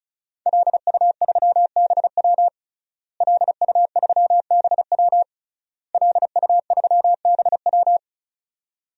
Morse code, heard as a keyed tone, about 36 wpm, 700 Hz. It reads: LU3BW LU3BW LU3BW
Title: DXCC-100 Call Signs - 5 Characters - Repeated 3x - 35wpm